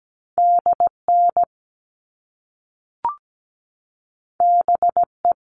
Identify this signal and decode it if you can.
Morse code, heard as a keyed tone, about 17 wpm, 700 Hz. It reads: DN BE
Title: All Cut Numbers - 2-Digit 17wpm